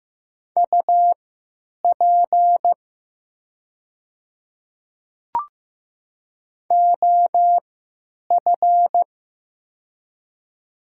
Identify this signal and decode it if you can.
Morse code, heard as a keyed tone, about 15 wpm, spaced as for 9 wpm, 700 Hz. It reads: UP OF
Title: 3x Character Spacing - Top 100 Words - 15wpm